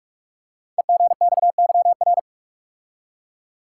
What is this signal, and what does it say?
Morse code, heard as a keyed tone, about 44 wpm, 700 Hz. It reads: EGXYR
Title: Code Groups - 5 Letters - 45wpm